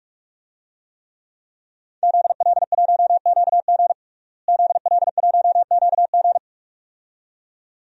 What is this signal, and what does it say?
Morse code, heard as a keyed tone, about 46 wpm, 700 Hz. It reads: ZL1QG ZL1QG
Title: DXCC-100 Call Signs - 5 Characters - Repeated 2x - 45wpm